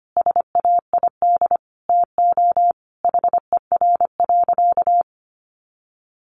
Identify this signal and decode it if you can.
Morse code, heard as a keyed tone, about 25 wpm, 700 Hz. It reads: SAID TO HER.
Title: Sentences from Top 1500 Words 25wpm